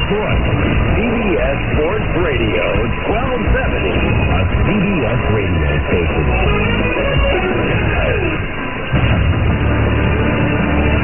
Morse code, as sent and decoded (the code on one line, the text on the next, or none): none